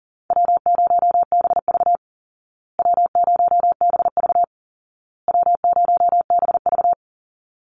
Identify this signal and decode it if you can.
Morse code, heard as a keyed tone, about 40 wpm, 700 Hz. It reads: W0BV W0BV W0BV
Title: Easy Call Signs Repeated 3x - 40wpm